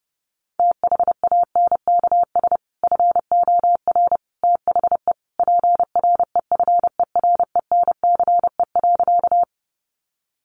Morse code, heard as a keyed tone, about 30 wpm, 700 Hz. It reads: THANKS FOR THE PREFERENCE.
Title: Sentences from Top 1500 Words 30wpm